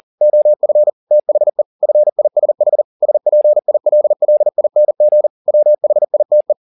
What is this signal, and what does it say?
Morse code, heard as a keyed tone, about 40 wpm, 600 Hz. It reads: OF THE FISH SPILLING WHITE
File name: the old man and the sea 2 40wpm